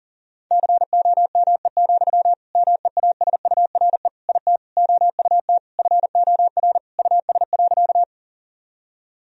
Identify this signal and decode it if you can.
Morse code, heard as a keyed tone, about 40 wpm, 700 Hz. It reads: COME, MEASURE IT OUT FOR US.
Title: Sentences from Top 1000 Words - Review Part 2 - 40wpm